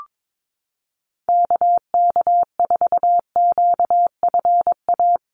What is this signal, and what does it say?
Morse code, heard as a keyed tone, about 22 wpm, 700 Hz. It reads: KK4QFA
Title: USA Call Signs Repeated 3x 22wpm